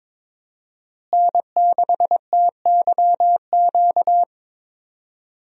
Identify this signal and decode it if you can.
Morse code, heard as a keyed tone, about 22 wpm, 700 Hz. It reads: N6TYQ
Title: USA Call Signs - 5 Characters - Repeated 2x - 22wpm